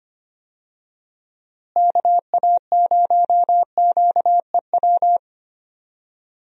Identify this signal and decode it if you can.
Morse code, heard as a keyed tone, about 25 wpm, 700 Hz. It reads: KA0QEW
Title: USA Call Signs - 6 Characters - Repeated 2x - 25wpm